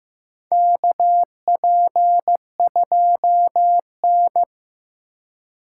Morse code, heard as a keyed tone, about 15 wpm, 700 Hz. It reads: KP2N
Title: K1USN SST QSO Practice - 15wpm